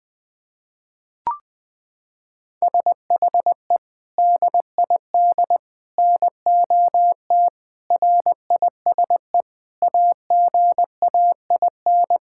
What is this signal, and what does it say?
Morse code, heard as a keyed tone, about 20 wpm, 700 Hz. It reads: SHE DID NOT RISE AGAIN
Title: Sentences from Top 1000 Words - Review Part 6 - 20wpm